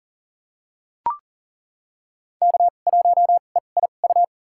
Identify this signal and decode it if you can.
Morse code, heard as a keyed tone, about 39 wpm, spaced as for 29 wpm, 700 Hz. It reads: K1EIU
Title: USA Call Signs - 5 Characters - 2x Character Spacing - 40wpm